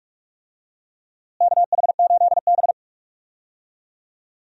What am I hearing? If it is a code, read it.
Morse code, heard as a keyed tone, about 46 wpm, 700 Hz. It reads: KH8B
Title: USA Call Signs - 4 Characters - Repeated 2x - 45wpm